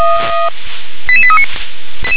Five-tone selective call (ZVEI), heard as quick runs of five tones, tones of about 70 ms. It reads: 88839
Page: SDR/ZVEI